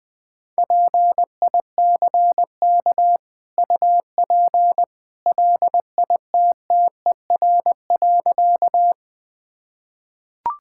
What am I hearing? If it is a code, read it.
Morse code, heard as a keyed tone, about 21 wpm, 700 Hz. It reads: PICK UP LITTER.
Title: Easy Sentences with Hints - 20wpm